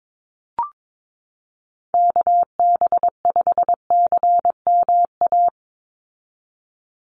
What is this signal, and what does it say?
Morse code, heard as a keyed tone, about 22 wpm, 700 Hz. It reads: KB5CMA
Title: USA Call Signs 22wpm